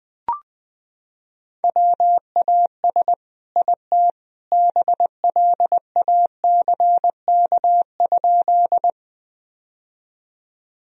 Morse code, heard as a keyed tone, about 19 wpm, 700 Hz. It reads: WAS IT BLACK?
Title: Sentences from Top 1000 Words - Review Part 2 - 20wpm